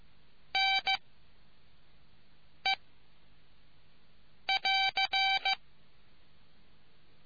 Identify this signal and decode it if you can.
Morse code, heard as a keyed tone, about 15 wpm, 770 Hz. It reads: N E <AR>